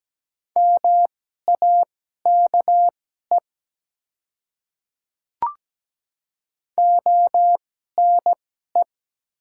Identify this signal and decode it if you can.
Morse code, heard as a keyed tone, about 17 wpm, spaced as for 12 wpm, 700 Hz. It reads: MAKE ONE
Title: 2x Character Spacing - Top 100 Words - 17wpm